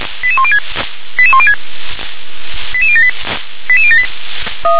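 Five-tone selective call (ZVEI), heard as quick runs of five tones, tones of about 70 ms. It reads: 88106 88106 88078 88078
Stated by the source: ZVEI five-tone selective call received on an SDR.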